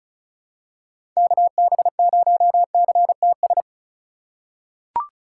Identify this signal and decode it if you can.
Morse code, heard as a keyed tone, about 36 wpm, 700 Hz. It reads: KB0CTS
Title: USA Call Signs - 6 Characters - 4x Character Spacing - 35wpm